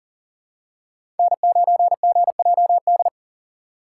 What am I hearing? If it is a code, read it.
Morse code, heard as a keyed tone, about 40 wpm, 700 Hz. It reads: N9GJD
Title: USA Call Signs - 5 Characters - 6x Character Spacing - 40wpm